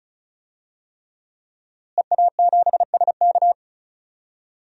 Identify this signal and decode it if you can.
Morse code, heard as a keyed tone, about 35 wpm, 700 Hz. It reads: EA7SK
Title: DXCC-100 Call Signs - 5 Characters - Repeated 3x - 35wpm